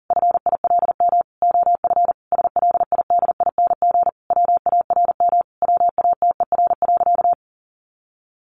Morse code, heard as a keyed tone, about 40 wpm, 700 Hz. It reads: FILM OF SLIDING WARM WATER.
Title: A Sound Of Thunder 40wpm